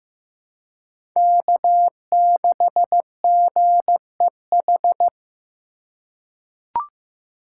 Morse code, heard as a keyed tone, about 15 wpm, 700 Hz. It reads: K6GEH
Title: USA Call Signs - 5 Characters - 2x Character Spacing - 15wpm